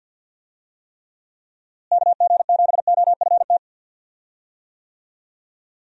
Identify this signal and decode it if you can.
Morse code, heard as a keyed tone, about 49 wpm, 700 Hz. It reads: KG7OFT
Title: USA Call Signs - 6 Characters - Repeated 3x - 50wpm